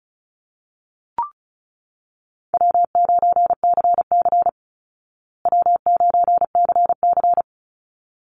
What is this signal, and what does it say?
Morse code, heard as a keyed tone, about 34 wpm, 700 Hz. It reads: W9CC W9CC
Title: Easy Call Signs Repeated 2x - 35wpm